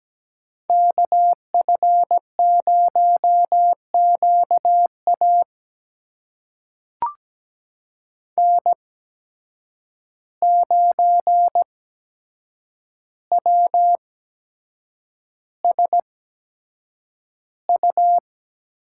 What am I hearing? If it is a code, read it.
Morse code, heard as a keyed tone, about 17 wpm, spaced as for 7 wpm, 700 Hz. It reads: KF0QA N9WSU
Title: USA Call Signs - 5 Characters - 8x Character Spacing - 17wpm